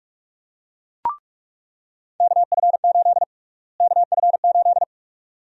Morse code, heard as a keyed tone, about 42 wpm, 700 Hz. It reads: KF8 KF8
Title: USA Call Sign Prefixes - Repeated 2x - 45wpm